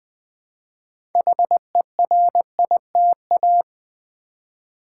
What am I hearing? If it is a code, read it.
Morse code, heard as a keyed tone, about 20 wpm, 700 Hz. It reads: HERITA
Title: Build A Word - 2 Letters At A Time - 8-Letter Words - 20wpm